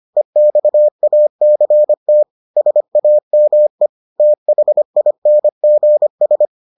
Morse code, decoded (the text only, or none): EXACT SAME THINGS